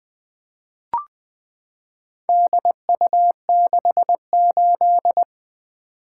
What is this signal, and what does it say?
Morse code, heard as a keyed tone, about 20 wpm, 700 Hz. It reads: DU68
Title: Four Letter-Number Combinations - 20wpm